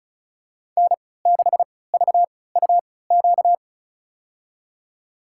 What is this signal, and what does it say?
Morse code, heard as a keyed tone, about 34 wpm, spaced as for 20 wpm, 700 Hz. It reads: N6VUQ
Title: USA Call Signs - 5 Characters - 3x Character Spacing - 35wpm